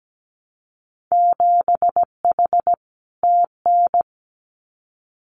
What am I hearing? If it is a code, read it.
Morse code, heard as a keyed tone, about 17 wpm, 700 Hz. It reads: 7H TN
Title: Winter Field Day Exchanges Encore 17wpm